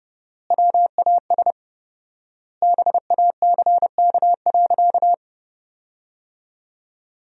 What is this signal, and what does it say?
Morse code, heard as a keyed tone, about 30 wpm, 700 Hz. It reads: WAS BACK.
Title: 4x Word-Spacing - Sentences from Top 500 Words - 30wpm